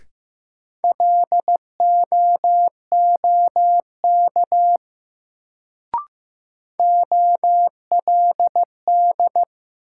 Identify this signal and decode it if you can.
Morse code, heard as a keyed tone, about 15 wpm, 700 Hz. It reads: LOOK OLD
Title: Sets of 2 Words - New Word Drill - 15wpm